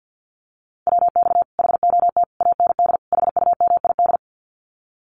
Morse code, heard as a keyed tone, about 50 wpm, 700 Hz. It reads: WX HOT AND HUMID
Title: Common QSO Phrases Encore 50wpm